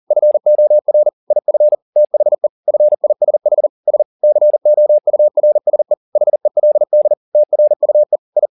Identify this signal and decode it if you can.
Morse code, heard as a keyed tone, about 40 wpm, 600 Hz. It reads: FOR IF THE FISH S COURSE HELD TRUE I